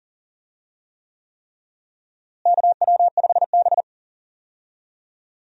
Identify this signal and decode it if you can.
Morse code, heard as a keyed tone, about 40 wpm, 700 Hz. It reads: KW5B
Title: USA Call Signs - 4 Characters - Repeated 3x - 40wpm